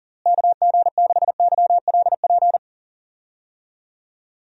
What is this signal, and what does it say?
Morse code, heard as a keyed tone, about 38 wpm, 700 Hz. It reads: KG6YLP
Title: USA Call Signs - 6 Characters - Repeated 3x - 40wpm